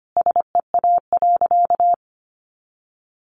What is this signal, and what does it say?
Morse code, heard as a keyed tone, about 25 wpm, 700 Hz. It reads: SEA.
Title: Sentences from Top 400 Words 25wpm